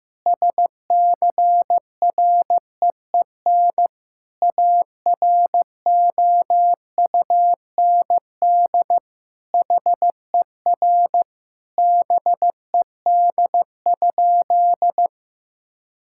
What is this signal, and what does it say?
Morse code, heard as a keyed tone, about 15 wpm, 700 Hz. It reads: SCREEN AROUND HER BED?
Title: Sentences from Top 1000 Words - Review Part 6 - 15wpm